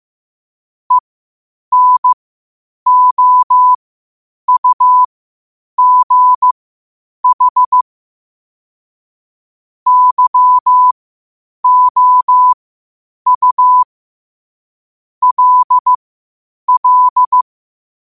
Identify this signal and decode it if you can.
Morse code, heard as a keyed tone, about 15 wpm, spaced as for 8 wpm, 1000 Hz. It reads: ENOUGH YOU LL
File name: 8wpm